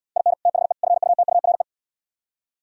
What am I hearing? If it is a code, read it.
Morse code, heard as a keyed tone, about 74 wpm, 700 Hz. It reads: IN THE FUTURE